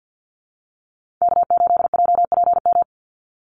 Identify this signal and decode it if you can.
Morse code, heard as a keyed tone, about 50 wpm, 700 Hz. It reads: K8JPM
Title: Call Signs - 50wpm